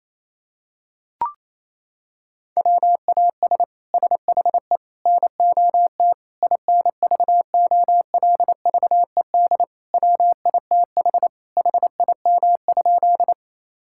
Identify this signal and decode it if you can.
Morse code, heard as a keyed tone, about 27 wpm, 700 Hz. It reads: WAS SHE NOT INVOLVED WITH HIM?
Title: Sentences from Top 1000 Words - Review Part 5 - 28wpm